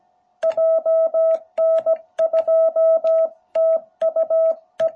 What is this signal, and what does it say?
Morse code, heard as a keyed tone, about 17 wpm, 660 Hz. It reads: JN2TUE